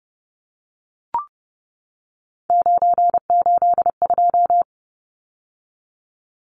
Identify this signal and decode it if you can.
Morse code, heard as a keyed tone, about 30 wpm, 700 Hz. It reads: 982
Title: Numbers - 3-Digit 30wpm